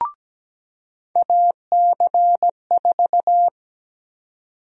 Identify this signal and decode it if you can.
Morse code, heard as a keyed tone, about 17 wpm, 700 Hz. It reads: AC4
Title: USA Call Sign Prefixes - 17wpm